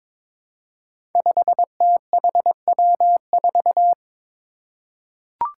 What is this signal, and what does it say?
Morse code, heard as a keyed tone, about 22 wpm, 700 Hz. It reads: 5THW4
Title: Five Challenging Letter-Number Combinations 22wpm